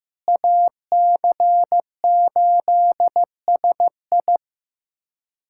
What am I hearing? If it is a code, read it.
Morse code, heard as a keyed tone, about 15 wpm, 700 Hz. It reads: AC8SI
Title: K1USN SST QSO Practice - 15wpm